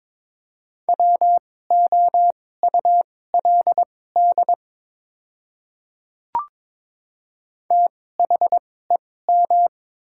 Morse code, heard as a keyed tone, about 22 wpm, spaced as for 16 wpm, 700 Hz. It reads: WOULD THEM
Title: 2x Character Spacing - Top 100 Words - 22wpm